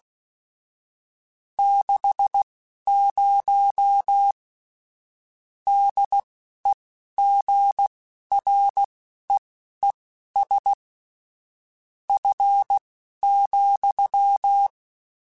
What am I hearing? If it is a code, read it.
Morse code, heard as a keyed tone, about 16 wpm, spaced as for 12 wpm, 800 Hz. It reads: 60 DEGREES F,
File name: Spaced Characters / Spaced Characters-mp3s / Weather 5 22 sp @15